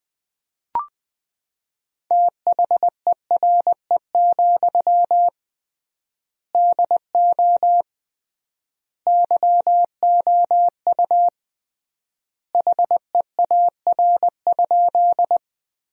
Morse code, heard as a keyed tone, about 20 wpm, 700 Hz. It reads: THERE, DO YOU HEAR?
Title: 3x Word-Spacing - Sentences from Top 300 Words - 20wpm